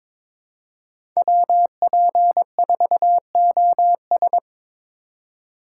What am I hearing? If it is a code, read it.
Morse code, heard as a keyed tone, about 22 wpm, 700 Hz. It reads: WP4OS
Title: USA Call Signs - 5 Characters - 2x Character Spacing - 22wpm